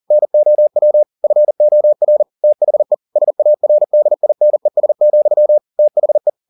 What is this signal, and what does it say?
Morse code, heard as a keyed tone, about 40 wpm, 600 Hz. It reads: NOW FOR THE SARDINES, THE